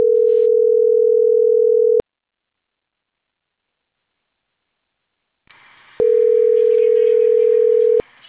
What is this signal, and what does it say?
Ringback tone (North America), 440 + 480 Hz.